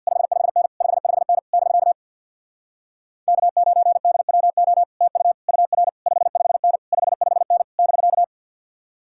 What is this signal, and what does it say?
Morse code, heard as a keyed tone, about 66 wpm, 700 Hz. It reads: 55N 55N <BK> K9DWO TU UR 55N 55N <BK>